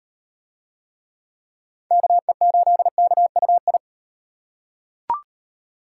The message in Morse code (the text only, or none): KE8KUI